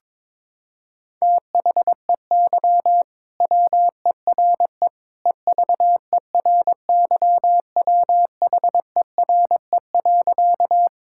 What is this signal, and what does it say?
Morse code, heard as a keyed tone, about 22 wpm, 700 Hz. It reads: THEY WERE EVERYWHERE.
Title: Sentences from Top 1600 Words 22wpm